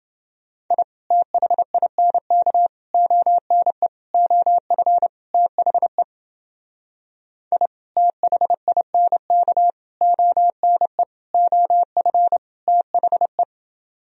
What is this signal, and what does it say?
Morse code, heard as a keyed tone, about 28 wpm, 700 Hz. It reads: I THINK ONE OF THE I THINK ONE OF THE